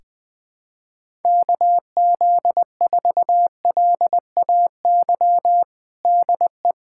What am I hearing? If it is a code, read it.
Morse code, heard as a keyed tone, about 20 wpm, 700 Hz. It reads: KZ4LAY DE
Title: USA Callsign DE Callsign - 6 Characters - 20wpm